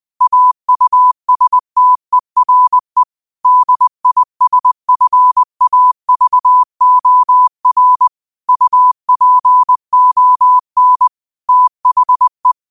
Morse code, heard as a keyed tone, about 20 wpm, 1000 Hz. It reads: AUSTERE DISFAVOR UPON THE